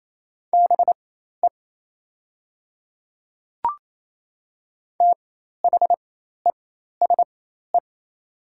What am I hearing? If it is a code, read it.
Morse code, heard as a keyed tone, about 28 wpm, spaced as for 13 wpm, 700 Hz. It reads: BE THESE
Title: 4x Character Spacing - Top 100 Words - 28wpm